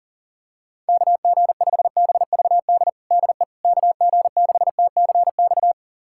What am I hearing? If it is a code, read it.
Morse code, heard as a keyed tone, about 40 wpm, 700 Hz. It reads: KG5BVD DE KG6TCX
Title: USA Callsign DE Callsign 40wpm